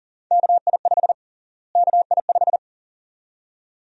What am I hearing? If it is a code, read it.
Morse code, heard as a keyed tone, about 40 wpm, 700 Hz. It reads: KI5 KI5
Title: USA Call Sign Prefixes - Repeated 2x - 40wpm